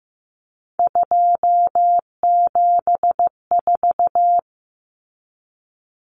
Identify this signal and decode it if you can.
Morse code, heard as a keyed tone, about 15 wpm, 700 Hz. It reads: 274
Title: Numbers - 3-Digit 15wpm